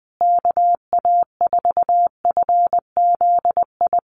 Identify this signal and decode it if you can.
Morse code, heard as a keyed tone, about 20 wpm, 700 Hz. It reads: KA4FZI
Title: USA Call Signs Repeated 2x 20wpm